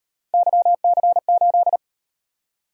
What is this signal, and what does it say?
Morse code, heard as a keyed tone, about 34 wpm, 700 Hz. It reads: YC8